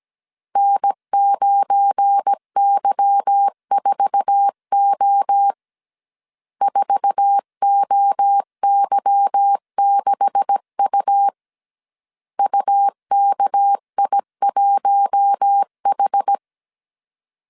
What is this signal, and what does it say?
Morse code, heard as a keyed tone, about 17 wpm, 800 Hz. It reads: N9Y4O 4OY6U UKI1H
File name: CW / DET / 1 9 1500C S A 16wpm 2828